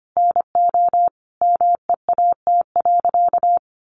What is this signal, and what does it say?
Morse code, heard as a keyed tone, about 25 wpm, 700 Hz. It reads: NO MEAT.
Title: Sentences from Top 1300 Words 25wpm